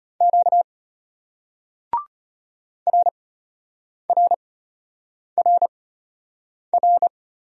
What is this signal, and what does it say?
Morse code, heard as a keyed tone, about 33 wpm, spaced as for 8 wpm, 700 Hz. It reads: Q RRRR